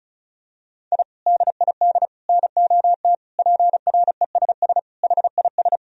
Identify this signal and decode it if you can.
Morse code, heard as a keyed tone, about 34 wpm, 700 Hz. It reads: I DID NOT PRESS HIS
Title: Sentences from Top 1000 Words - Review Part 1 - 35wpm